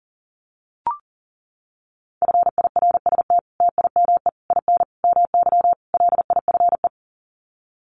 Morse code, heard as a keyed tone, about 37 wpm, 700 Hz. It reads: FIRST TIME IN MY LIFE
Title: Sets of 5 Words Encore2 40wpm